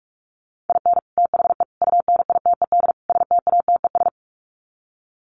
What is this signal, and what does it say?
Morse code, heard as a keyed tone, about 44 wpm, 700 Hz. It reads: IN THE UNITED STATES
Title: Sets of 4 Words 45wpm